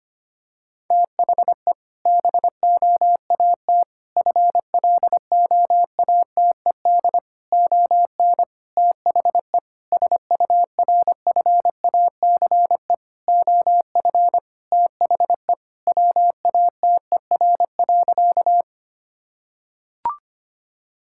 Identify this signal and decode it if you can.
Morse code, heard as a keyed tone, about 25 wpm, 700 Hz. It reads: THE BOAT FLOATED ON THE SURFACE OF THE WATER.
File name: practice / 25wpm / Sentences with 4-Word Phrases - 25wpm